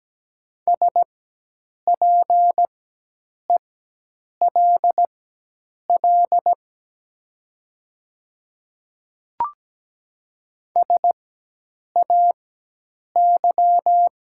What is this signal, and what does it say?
Morse code, heard as a keyed tone, about 17 wpm, spaced as for 8 wpm, 700 Hz. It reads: SPELL SAY
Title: 4x Character Spacing - Top 200 Words - 17wpm